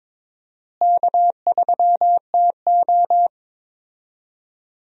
Morse code, heard as a keyed tone, about 22 wpm, 700 Hz. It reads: K3TO
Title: USA Call Signs - 4 Characters - Repeated 3x - 22wpm